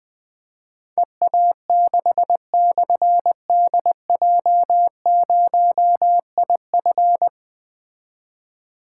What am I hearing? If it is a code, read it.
Morse code, heard as a keyed tone, about 20 wpm, 700 Hz. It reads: EA6/DJ0IF
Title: Difficult Call Signs 20wpm